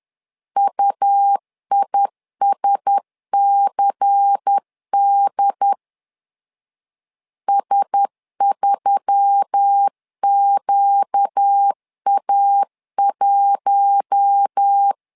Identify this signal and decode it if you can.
Morse code, heard as a keyed tone, about 10 wpm, 800 Hz. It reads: UISCD S3QA1
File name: LEZIONI / 5 LBVC / 5 LBVC 10wpm 2349